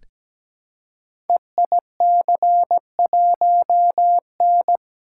Morse code, heard as a keyed tone, about 17 wpm, 700 Hz. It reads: EIC1N
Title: Code Groups - 5 Letter:Numbers - 17wpm